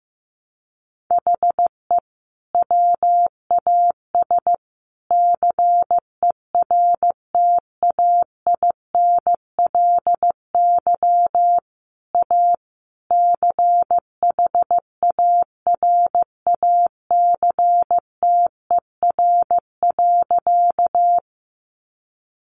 Morse code, heard as a keyed tone, about 15 wpm, 700 Hz. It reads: HE WAS CERTAINLY A CHARACTER.
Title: Sentences from Top 1200 Words 15wpm